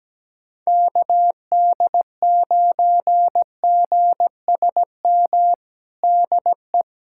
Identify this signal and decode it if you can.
Morse code, heard as a keyed tone, about 17 wpm, 700 Hz. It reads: KD9GSM DE
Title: USA Callsign DE Callsign 17wpm